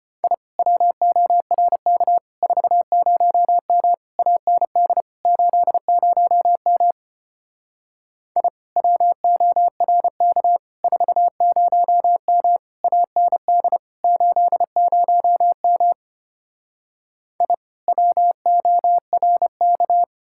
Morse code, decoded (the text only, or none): I WORK 40M AND 80M I WORK 40M AND 80M I WORK